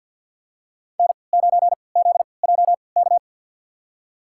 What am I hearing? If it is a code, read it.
Morse code, heard as a keyed tone, about 52 wpm, spaced as for 28 wpm, 700 Hz. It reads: N9ZJK
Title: USA Call Signs - 5 Characters - 3x Character Spacing - 50wpm